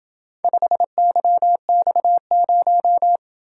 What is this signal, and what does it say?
Morse code, heard as a keyed tone, about 27 wpm, 700 Hz. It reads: 5YX0